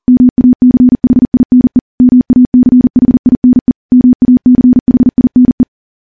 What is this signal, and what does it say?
Morse code, heard as a keyed tone, about 39 wpm, 250 Hz. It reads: MACHINE MACHINE MACHINE
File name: morse / morse-training-40WPM